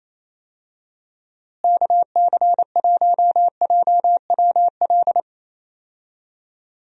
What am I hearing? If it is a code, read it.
Morse code, heard as a keyed tone, about 27 wpm, 700 Hz. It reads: KC1JWL
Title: USA Call Signs - 6 Characters - Repeated 2x - 28wpm